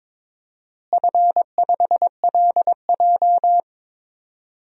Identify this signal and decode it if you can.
Morse code, heard as a keyed tone, about 22 wpm, 700 Hz. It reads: F5LJ
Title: DXCC-100 Call Signs - 4 Characters - Repeated 2x - 22wpm